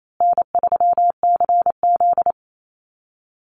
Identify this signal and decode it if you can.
Morse code, heard as a keyed tone, about 28 wpm, 700 Hz. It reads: N3CZ
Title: Easy Call Signs Repeated 3x - 28wpm